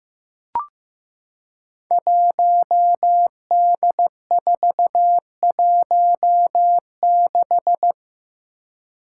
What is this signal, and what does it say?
Morse code, heard as a keyed tone, about 15 wpm, 700 Hz. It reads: 1D416
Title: Five Challenging Letter-Number Combinations 15wpm